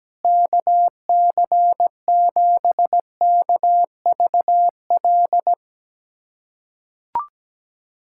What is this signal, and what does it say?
Morse code, heard as a keyed tone, about 17 wpm, 700 Hz. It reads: KC7KVL
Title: USA Callsign DE Callsign - 6 Characters - 17wpm